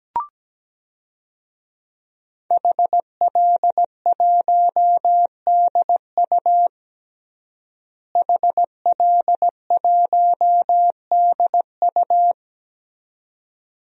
Morse code, heard as a keyed tone, about 17 wpm, 700 Hz. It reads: HL1DU HL1DU
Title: DXCC-100 Call Signs - 5 Characters - Repeated 2x - 17wpm